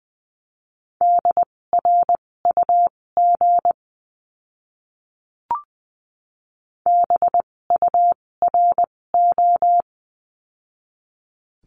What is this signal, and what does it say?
Morse code, heard as a keyed tone, about 20 wpm, spaced as for 16 wpm, 700 Hz. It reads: DRUG BURO
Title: 256 Lesson 42a - Words 12-20wpm